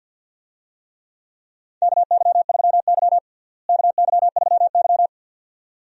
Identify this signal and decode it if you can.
Morse code, heard as a keyed tone, about 49 wpm, 700 Hz. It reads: KY3Y KY3Y
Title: USA Call Signs - 4 Characters - Repeated 2x - 50wpm